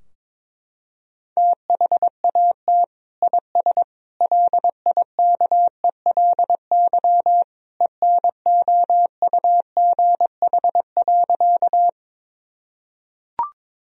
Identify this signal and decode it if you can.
Morse code, heard as a keyed tone, about 22 wpm, 700 Hz. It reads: THAT IS LIKELY ENOUGH.
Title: Sentences from Top 1000 Words - Review Part 4 - 22wpm